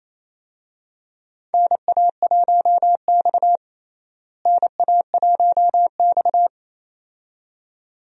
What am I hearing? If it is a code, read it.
Morse code, heard as a keyed tone, about 28 wpm, 700 Hz. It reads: NA1X NA1X
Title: USA Call Signs - 4 Characters - Repeated 2x - 28wpm